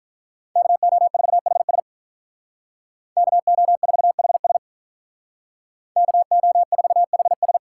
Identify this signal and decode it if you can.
Morse code, heard as a keyed tone, about 43 wpm, 700 Hz. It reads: KO4HS KO4HS KO4HS
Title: Speed Racing - USA Call Signs 35wpm